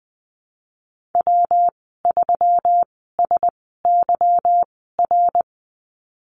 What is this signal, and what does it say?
Morse code, heard as a keyed tone, about 20 wpm, spaced as for 15 wpm, 700 Hz. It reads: W3SYR
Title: 179 Lesson 29g - Slash - Call Signs 20wpm